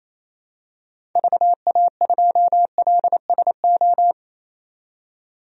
Morse code, heard as a keyed tone, about 28 wpm, 700 Hz. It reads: VA2LSO
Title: DXCC-100 Call Signs - 6 Characters - 28wpm